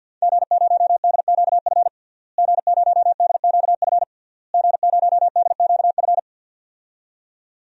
Morse code, heard as a keyed tone, about 49 wpm, 700 Hz. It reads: G0DQF G0DQF G0DQF
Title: DXCC-100 Call Signs - 5 Characters - Repeated 3x - 50wpm